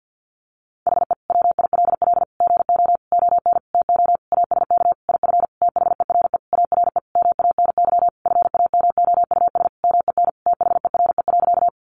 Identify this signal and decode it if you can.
Morse code, heard as a keyed tone, about 50 wpm, 700 Hz. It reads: HE WILL GO ON TO ASK IF THERE ARE MANY FAMOUS MEN THERE.
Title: Sentences from Top 1500 Words 50wpm